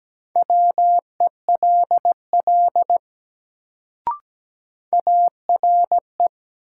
Morse code, heard as a keyed tone, about 17 wpm, 700 Hz. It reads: WELL ARE
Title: Sets of 2 Words - New Word Drill - 17wpm